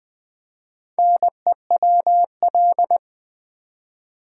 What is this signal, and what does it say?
Morse code, heard as a keyed tone, about 20 wpm, 700 Hz. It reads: NEWL
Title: Build A Word - 1 Letter At A Time - 5-Letter Words - 20wpm